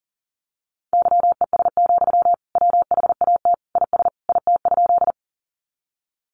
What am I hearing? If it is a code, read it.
Morse code, heard as a keyed tone, about 40 wpm, 700 Hz. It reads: YES, WHAT IS IT?
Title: Sentences from Top 500 Words 40wpm